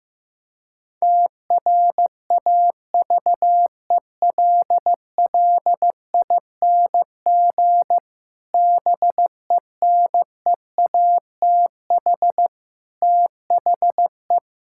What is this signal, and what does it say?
Morse code, heard as a keyed tone, about 16 wpm, 700 Hz. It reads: TRAVELLING BENEATH THE
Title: Pangram Challenge v2 - 15wpm